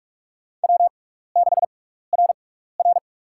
Morse code, heard as a keyed tone, about 48 wpm, spaced as for 16 wpm, 700 Hz. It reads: W6RR
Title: USA Call Signs - 4 Characters - 6x Character Spacing - 45wpm